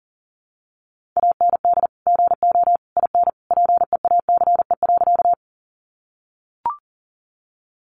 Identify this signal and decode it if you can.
Morse code, heard as a keyed tone, about 40 wpm, 700 Hz. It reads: AND GO IN PEACE.
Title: Sentences from Top 1500 Words 40wpm